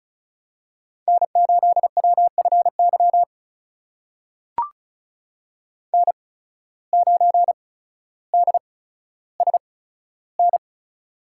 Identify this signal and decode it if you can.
Morse code, heard as a keyed tone, about 36 wpm, spaced as for 15 wpm, 700 Hz. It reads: N8WFY N9DSN
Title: USA Call Signs - 5 Characters - 8x Character Spacing - 35wpm